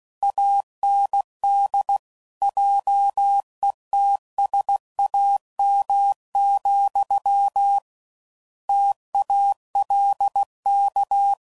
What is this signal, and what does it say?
Morse code, heard as a keyed tone, about 16 wpm, 800 Hz. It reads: AND JETSAM, TALK
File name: Mixed / Misc / WordPairs